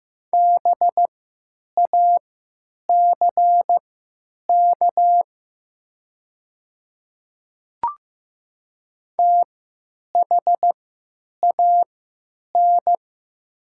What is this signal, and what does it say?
Morse code, heard as a keyed tone, about 15 wpm, spaced as for 9 wpm, 700 Hz. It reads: BACK THAN
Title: 3x Character Spacing - Top 300 Words - 15wpm